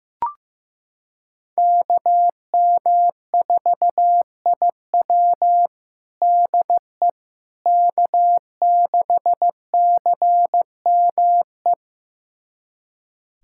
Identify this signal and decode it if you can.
Morse code, heard as a keyed tone, about 15 wpm, 700 Hz. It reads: KM4IW DE K6CME
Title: USA Callsign DE Callsign - 5 Characters - 15wpm